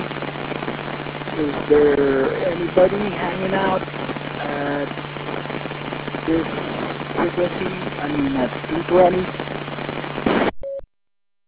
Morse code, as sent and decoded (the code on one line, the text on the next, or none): none